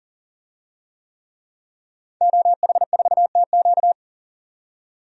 Morse code, heard as a keyed tone, about 40 wpm, 700 Hz. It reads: OH4TQ
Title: DXCC-100 Call Signs - 5 Characters - 40wpm